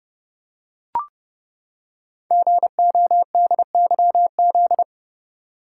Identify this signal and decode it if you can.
Morse code, heard as a keyed tone, about 30 wpm, 700 Hz. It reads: GODYZ
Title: Code Groups - 5 Letters - 30wpm